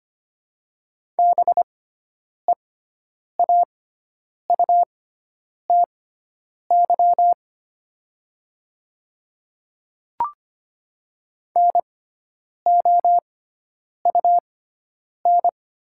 Morse code, heard as a keyed tone, about 25 wpm, spaced as for 9 wpm, 700 Hz. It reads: BEAUTY NOUN
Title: 6x Character Spacing - Top 500 Words - 25wpm